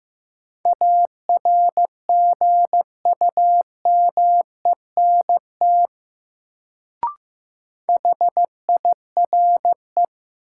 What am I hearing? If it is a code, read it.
Morse code, heard as a keyed tone, about 15 wpm, 700 Hz. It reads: ARGUMENT HIRE
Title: Sentences from Top 1400 Words - New Word Drill - 15wpm